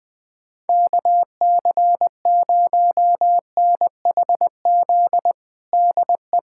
Morse code, heard as a keyed tone, about 20 wpm, 700 Hz. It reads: KC0NHZ DE
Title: USA Callsign DE Callsign - 6 Characters - 20wpm